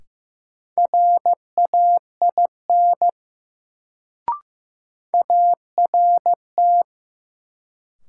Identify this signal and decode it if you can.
Morse code, heard as a keyed tone, about 15 wpm, 700 Hz. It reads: RAIN ART
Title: Sentences from Top 600 Words - New Word Drill - 15wpm